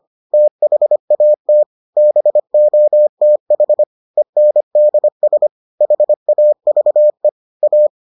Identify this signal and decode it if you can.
Morse code, heard as a keyed tone, about 25 wpm, 600 Hz. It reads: THAT BOTH ENDS HAVE A